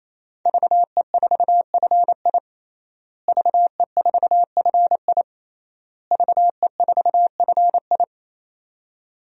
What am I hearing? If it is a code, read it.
Morse code, heard as a keyed tone, about 27 wpm, 700 Hz. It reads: VE4FI VE4FI VE4FI
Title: DXCC-100 Call Signs - 5 Characters - Repeated 3x - 28wpm